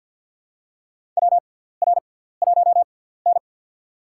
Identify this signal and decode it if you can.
Morse code, heard as a keyed tone, about 52 wpm, spaced as for 17 wpm, 700 Hz. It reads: WR1N
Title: USA Call Signs - 4 Characters - 6x Character Spacing - 50wpm